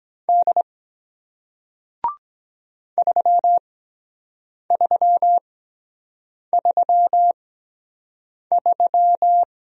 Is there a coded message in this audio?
Yes: Morse code from 0.3 to 9.4 s, heard as a keyed tone, about 23 wpm, spaced as for 7 wpm, 700 Hz. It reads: D 3333